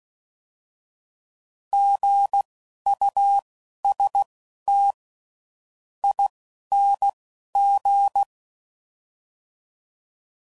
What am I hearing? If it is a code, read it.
Morse code, heard as a keyed tone, about 16 wpm, spaced as for 10 wpm, 800 Hz. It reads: GUSTING
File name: Prefixes / By First Letter